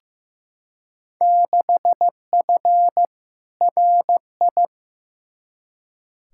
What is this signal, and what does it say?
Morse code, heard as a keyed tone, about 15 wpm, 700 Hz. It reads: 6F RI
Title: Field Day Exchanges Encore - 15wpm